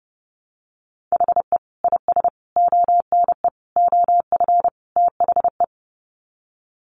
Morse code, heard as a keyed tone, about 30 wpm, 700 Hz. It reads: HE IS ONE OF THE
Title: Sets of 5 Words Encore 30wpm